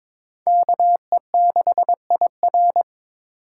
Morse code, heard as a keyed tone, about 22 wpm, 700 Hz. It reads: KE6IR
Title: USA Call Signs - 2x Character Spacing - 22wpm